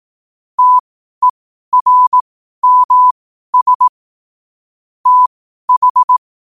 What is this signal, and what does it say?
Morse code, heard as a keyed tone, about 18 wpm, spaced as for 12 wpm, 1000 Hz. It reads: TERMS TH
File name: qotdfiles / 12wpm